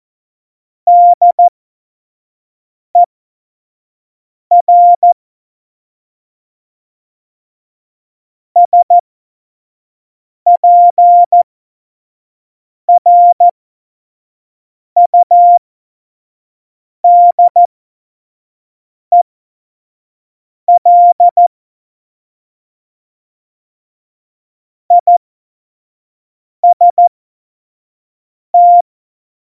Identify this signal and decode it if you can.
Morse code, heard as a keyed tone, about 14 wpm, spaced as for 5 wpm, 700 Hz. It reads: DER SPRUDEL IST
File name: Morsekurs / Witze / Witz 24.Dezember 2019, 25 F70